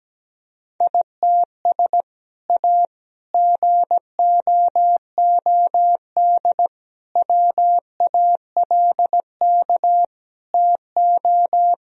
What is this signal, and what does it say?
Morse code, heard as a keyed tone, about 17 wpm, 700 Hz. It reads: ITS A GOOD WALK TO
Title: Sentences from Top 1000 Words - Review Part 1 - 17wpm